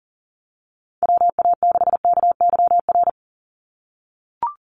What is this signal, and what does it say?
Morse code, heard as a keyed tone, about 40 wpm, 700 Hz. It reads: WA6KYR
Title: NA SOTA Chaser Callsigns 40wpm